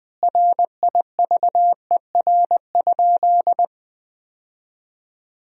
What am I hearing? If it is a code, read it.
Morse code, heard as a keyed tone, about 20 wpm, 700 Hz. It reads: RIVER?
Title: 2x Word-Spacing - Sentences from Top 300 Words - 20wpm